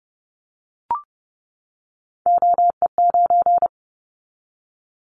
Morse code, heard as a keyed tone, about 30 wpm, 700 Hz. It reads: OE9
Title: Callsign Prefixes for Top 50 DXCC - 30wpm